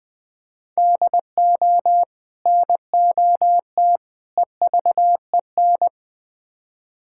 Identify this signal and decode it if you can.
Morse code, heard as a keyed tone, about 20 wpm, 700 Hz. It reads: DO NOT EVEN
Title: Build A Phrase - 4-Word Phrases - 20wpm